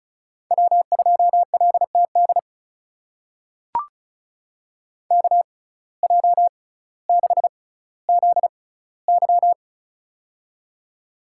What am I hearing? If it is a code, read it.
Morse code, heard as a keyed tone, about 36 wpm, spaced as for 18 wpm, 700 Hz. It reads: W2LTD KJ6ZY
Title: USA Call Signs - 5 Characters - 6x Character Spacing - 35wpm